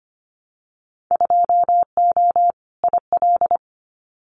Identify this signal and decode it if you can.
Morse code, heard as a keyed tone, about 25 wpm, 700 Hz. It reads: 2O IL
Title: Winter Field Day Exchanges 25wpm